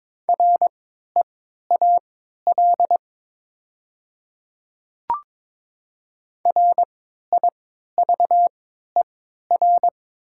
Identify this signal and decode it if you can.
Morse code, heard as a keyed tone, about 22 wpm, spaced as for 12 wpm, 700 Hz. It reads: REAL RIVER
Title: 3x Character Spacing - Top 500 Words - 22wpm